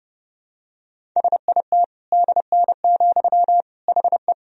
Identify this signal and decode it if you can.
Morse code, heard as a keyed tone, about 30 wpm, 700 Hz. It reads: SIT DN, HE
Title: Sentences from Top 500 Words and Abbreviations 30wpm